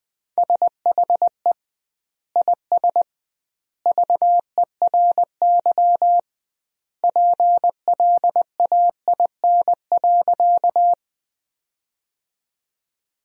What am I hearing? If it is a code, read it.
Morse code, heard as a keyed tone, about 21 wpm, 700 Hz. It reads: SHE IS VERY PLAIN.
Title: 2x Word-Spacing - Sentences from Top 400 Words - 20wpm